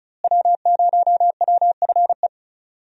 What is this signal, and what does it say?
Morse code, heard as a keyed tone, about 36 wpm, 700 Hz. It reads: W0WFE
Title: USA Call Signs - 5 Characters - Repeated 3x - 35wpm